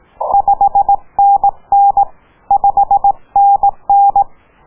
Morse code, heard as a keyed tone, about 17 wpm, 750 Hz. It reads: 5NN 5NN